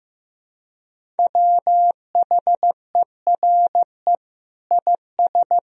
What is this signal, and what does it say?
Morse code, heard as a keyed tone, about 16 wpm, 700 Hz. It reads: WHERE IS
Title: Sentences from Top 1000 Words - Review Part 4 - 15wpm